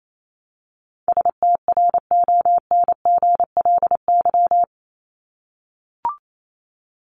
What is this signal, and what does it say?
Morse code, heard as a keyed tone, about 28 wpm, 700 Hz. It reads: STRONGLY
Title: LY Suffix - Top 100 Words with Base Words 28wpm